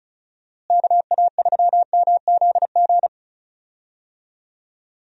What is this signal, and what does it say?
Morse code, heard as a keyed tone, about 36 wpm, 700 Hz. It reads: KA3MZG
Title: USA Call Signs - 6 Characters - Repeated 3x - 35wpm